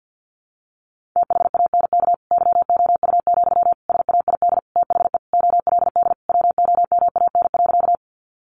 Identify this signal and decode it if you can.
Morse code, heard as a keyed tone, about 50 wpm, 700 Hz. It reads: THANK YOU, SAID THE OLD WOMAN.
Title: Sentences from Top 1100 Words 50wpm